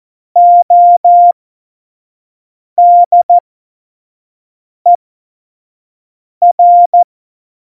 Morse code, heard as a keyed tone, about 14 wpm, spaced as for 5 wpm, 700 Hz. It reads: ODER
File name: Morsekurs / Witze / Witz 14. November 2017, 25 F70